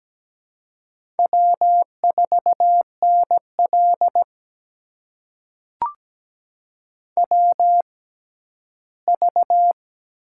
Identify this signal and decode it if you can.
Morse code, heard as a keyed tone, about 17 wpm, spaced as for 11 wpm, 700 Hz. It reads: W4NL WV
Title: USA Call Signs - 4 Characters - 6x Character Spacing - 17wpm